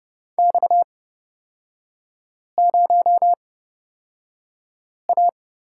Morse code, heard as a keyed tone, about 29 wpm, spaced as for 5 wpm, 700 Hz. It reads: X0A
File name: Lesson 38e - Letter q - Three Characters 30wpm